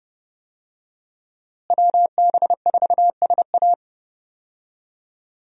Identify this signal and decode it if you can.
Morse code, heard as a keyed tone, about 30 wpm, 700 Hz. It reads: WB4SA
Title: USA Call Signs - 5 Characters - Repeated 2x - 30wpm